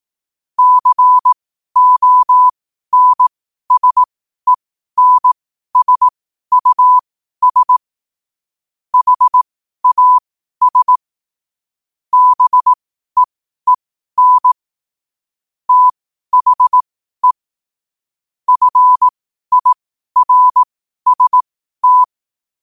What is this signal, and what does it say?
Morse code, heard as a keyed tone, about 18 wpm, spaced as for 12 wpm, 1000 Hz. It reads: CONSENSUS HAS BEEN THE FIRST
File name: qotdfiles / 12wpm